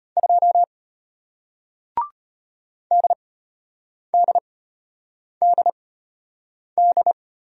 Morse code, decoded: 2 DDDD

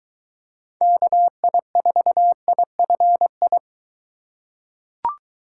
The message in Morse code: KI4IFI

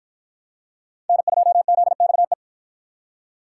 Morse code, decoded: N2ZXE

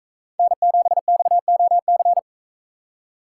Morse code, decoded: N7XOC

